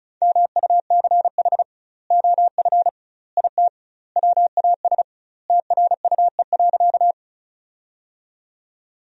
Morse code, heard as a keyed tone, about 36 wpm, 700 Hz. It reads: MUCH OF IT WAS TRUE.
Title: 2x Word-Spacing - Sentences from Top 400 Words - 35wpm